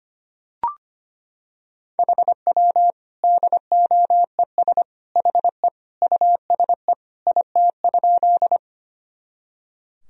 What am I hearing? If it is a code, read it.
Morse code, heard as a keyed tone, about 25 wpm, 700 Hz. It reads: HW DOES HE USE IT?
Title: Sentences from Top 200 Words and Abbreviations 25wpm